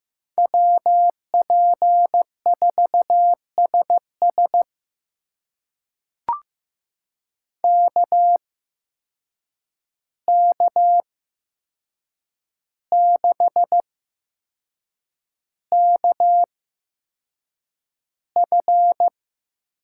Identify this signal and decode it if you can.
Morse code, heard as a keyed tone, about 15 wpm, spaced as for 6 wpm, 700 Hz. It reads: WP4SS KK6KF
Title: USA Call Signs - 5 Characters - 8x Character Spacing - 15wpm